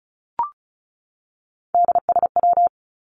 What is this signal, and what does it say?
Morse code, heard as a keyed tone, about 34 wpm, 700 Hz. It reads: DSW
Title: Abbreviations 35wpm